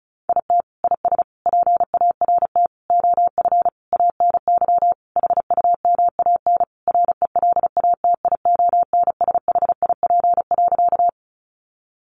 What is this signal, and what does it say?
Morse code, heard as a keyed tone, about 34 wpm, 700 Hz. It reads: IT IS PART OF ANY HUMAN RELATIONSHIP.
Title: Sentences from Top 1100 Words 35wpm